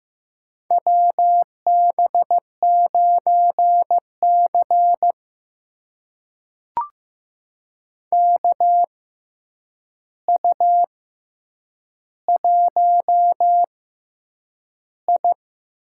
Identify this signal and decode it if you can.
Morse code, heard as a keyed tone, about 15 wpm, spaced as for 8 wpm, 700 Hz. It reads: WB9C KU1I
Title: USA Call Signs - 6x Character Spacing - 15wpm